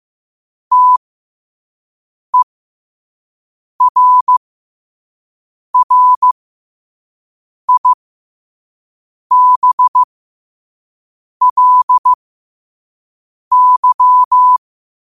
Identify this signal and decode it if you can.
Morse code, heard as a keyed tone, about 15 wpm, spaced as for 5 wpm, 1000 Hz. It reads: TERRIBLY
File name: 5wpm